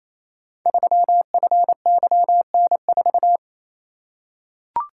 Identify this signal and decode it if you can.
Morse code, heard as a keyed tone, about 27 wpm, 700 Hz. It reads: 3FYN4
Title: Code Groups - 5 Letter:Numbers - 28wpm